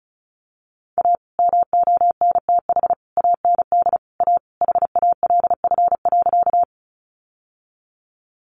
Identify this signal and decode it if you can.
Morse code, heard as a keyed tone, about 34 wpm, 700 Hz. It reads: A MONTH AND A HALF.
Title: Sentences from Top 800 Words 35wpm